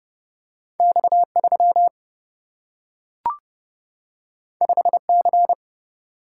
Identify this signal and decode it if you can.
Morse code, heard as a keyed tone, about 29 wpm, 700 Hz. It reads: X3 5C
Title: Single-Letter Single-Number 30wpm